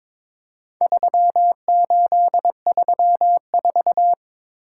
Morse code, heard as a keyed tone, about 22 wpm, 700 Hz. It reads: 3834
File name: practice / 22wpm / Four Letter-Number Combinations - 22wpm